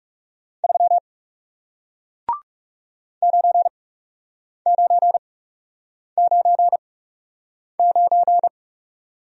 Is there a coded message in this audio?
Yes: Morse code from 0.6 to 8.5 s, heard as a keyed tone, about 38 wpm, spaced as for 8 wpm, 700 Hz. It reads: 3 9999